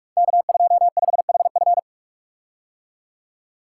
Morse code, heard as a keyed tone, about 44 wpm, 700 Hz. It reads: K25HF
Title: Code Groups - 5 Letter:Numbers - 45wpm